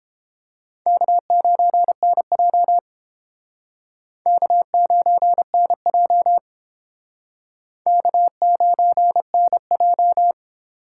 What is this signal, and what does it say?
Morse code, heard as a keyed tone, about 29 wpm, 700 Hz. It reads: K9NJ K9NJ K9NJ